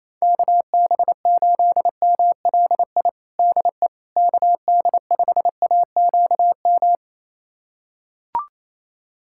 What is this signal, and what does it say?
Morse code, heard as a keyed tone, about 28 wpm, 700 Hz. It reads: KB8MLI DE KD5AQM
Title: USA Callsign DE Callsign 28wpm